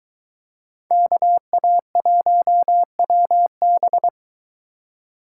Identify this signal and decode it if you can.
Morse code, heard as a keyed tone, about 20 wpm, 700 Hz. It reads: KA1WB